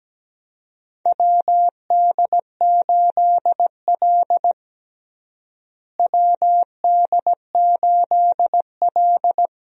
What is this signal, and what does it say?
Morse code, heard as a keyed tone, about 17 wpm, 700 Hz. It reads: WD8L WD8L
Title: USA Call Signs - 4 Characters - Repeated 2x - 17wpm